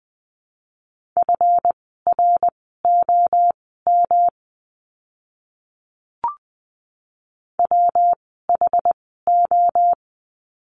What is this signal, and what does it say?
Morse code, heard as a keyed tone, about 20 wpm, spaced as for 15 wpm, 700 Hz. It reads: FROM WHO
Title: 230 Lesson 37f - Letter x - Words 20wpm